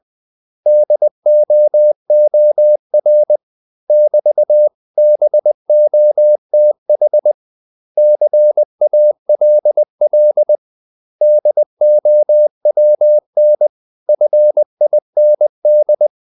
Morse code, heard as a keyed tone, about 20 wpm, 600 Hz. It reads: DOOR <BT>BOTH CALL DOWN FIND
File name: CW / Practice / 500MostCommonEnglishWords20